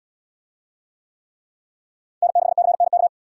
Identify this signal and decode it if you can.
Morse code, heard as a keyed tone, about 74 wpm, 690 Hz. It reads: N6CNK